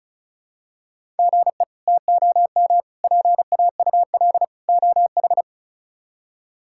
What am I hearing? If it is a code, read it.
Morse code, heard as a keyed tone, about 36 wpm, 700 Hz. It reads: GE TOM PAUL OH
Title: K1USN SST QSO Practice - 35wpm